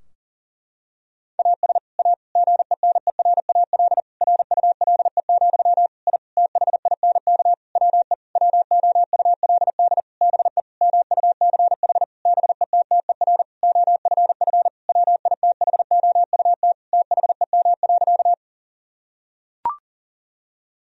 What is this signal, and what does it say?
Morse code, heard as a keyed tone, about 40 wpm, 700 Hz. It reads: AS A GENERAL RULE, I THINK WE WOULD BE MUCH BETTER OFF WITHOUT THEM.
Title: Sentences from Top 1000 Words - Review Part 3 - 40wpm